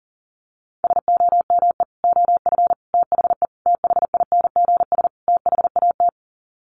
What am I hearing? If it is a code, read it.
Morse code, heard as a keyed tone, about 40 wpm, 700 Hz. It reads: SOME OF THE THINGS THAT
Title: Sets of 5 Words 40wpm